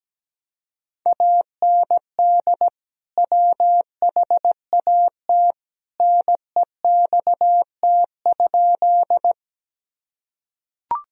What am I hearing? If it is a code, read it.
Morse code, heard as a keyed tone, about 17 wpm, 700 Hz. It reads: AND WHAT NEXT?
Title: Sentences from Top 1000 Words - Review Part 1 - 17wpm